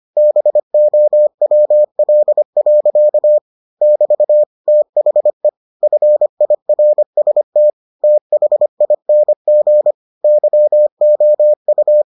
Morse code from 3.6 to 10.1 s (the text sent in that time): <BT> THE FIRST THING